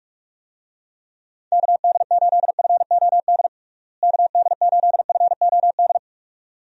Morse code, heard as a keyed tone, about 47 wpm, 700 Hz. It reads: KD8FOD KD8FOD
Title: USA Call Signs - 6 Characters - Repeated 2x - 45wpm